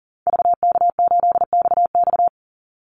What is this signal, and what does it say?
Morse code, heard as a keyed tone, about 40 wpm, 700 Hz. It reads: VK8XX
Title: Call Signs - 40wpm